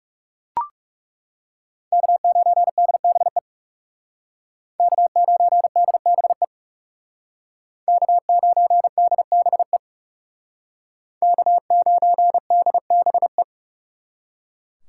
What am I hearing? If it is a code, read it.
Morse code, heard as a keyed tone, about 36 wpm, 700 Hz. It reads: K9DBE K9DBE K9DBE K9DBE
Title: Speed Racing - USA 1x3 Call Signs 30wpm